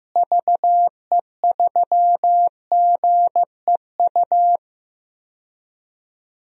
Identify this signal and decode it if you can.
Morse code, heard as a keyed tone, about 15 wpm, 700 Hz. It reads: VE3GEU
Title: DXCC-100 Call Signs - 6 Characters - Repeated 3x - 15wpm